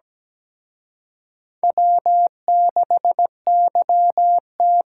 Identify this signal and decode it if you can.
Morse code, heard as a keyed tone, about 20 wpm, 700 Hz. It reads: W6YT
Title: Speed Racing - USA 1x2 Call Signs 17wpm